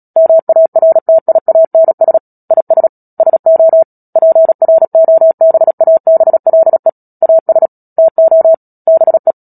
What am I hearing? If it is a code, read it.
Morse code, heard as a keyed tone, about 37 wpm, 650 Hz. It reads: MARTIANS IS SO PROBABLE AS TO BE
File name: Book Two / 10-Chapter 10 - Epilogue - 36 WPM